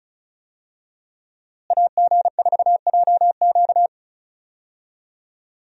Morse code, heard as a keyed tone, about 36 wpm, 700 Hz. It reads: AG4JQ
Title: USA Call Signs - 5 Characters - Repeated 2x - 35wpm